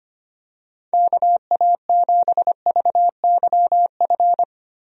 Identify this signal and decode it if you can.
Morse code, heard as a keyed tone, about 25 wpm, 700 Hz. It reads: KA7VYF
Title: USA Call Signs - 6 Characters - Repeated 3x - 25wpm